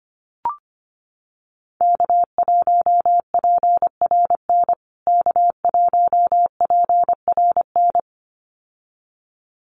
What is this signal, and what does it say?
Morse code, heard as a keyed tone, about 25 wpm, 700 Hz. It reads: K1PRN K1PRN
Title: USA Call Signs Repeated 2x 25wpm